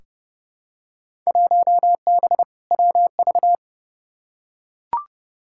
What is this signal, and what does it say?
Morse code, heard as a keyed tone, about 29 wpm, 700 Hz. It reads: 1B WV
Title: Field Day Exchanges Encore - 30wpm